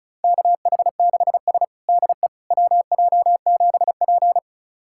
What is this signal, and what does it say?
Morse code, heard as a keyed tone, about 36 wpm, 700 Hz. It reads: KH6S DE WJ7P
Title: USA Callsign DE Callsign - 4 Characters - 35wpm